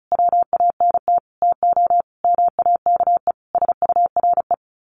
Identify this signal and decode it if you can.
Morse code, heard as a keyed tone, about 34 wpm, 700 Hz. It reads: WANT TO MAKE SURE
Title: Sets of 5 Words Encore2 35wpm